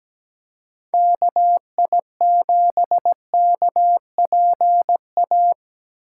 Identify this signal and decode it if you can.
Morse code, heard as a keyed tone, about 17 wpm, 700 Hz. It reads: KI7KPA
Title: USA Call Signs - 8x Character Spacing - 17wpm